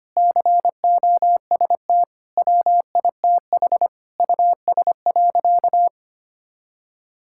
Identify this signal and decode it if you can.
Morse code, heard as a keyed tone, about 25 wpm, 700 Hz. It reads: COST WITH US.
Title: Sentences from Top 1000 Words - Review Part 6 - 25wpm